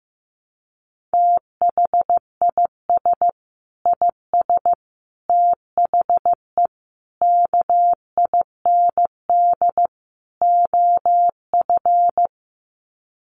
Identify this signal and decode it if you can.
Morse code, heard as a keyed tone, about 15 wpm, 700 Hz. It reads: THIS IS THE KIND OF
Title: Sets of 5 Words Encore 15wpm